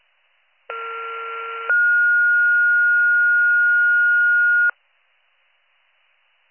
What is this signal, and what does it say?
Two-tone page, heard as a one-second tone, then a three-second tone, about 1370 then 1430 Hz. It is heard from 0.7 to 4.7 s.